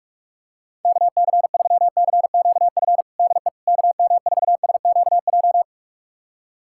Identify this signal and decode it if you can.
Morse code, heard as a keyed tone, about 46 wpm, 700 Hz. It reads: KC3CQF DE KM4SQJ
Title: USA Callsign DE Callsign - 6 Characters - 45wpm